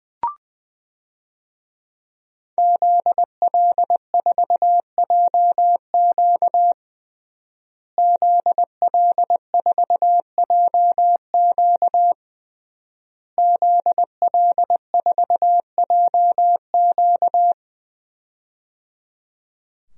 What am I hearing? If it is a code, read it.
Morse code, heard as a keyed tone, about 20 wpm, 700 Hz. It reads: ZL4JQ ZL4JQ ZL4JQ
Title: DXCC-100 Call Signs - 5 Characters - Repeated 3x - 20wpm